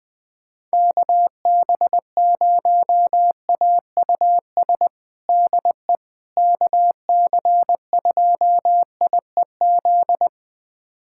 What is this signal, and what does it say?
Morse code, heard as a keyed tone, about 20 wpm, 700 Hz. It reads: KB0AUS DE KC2IEZ
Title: USA Callsign DE Callsign 20wpm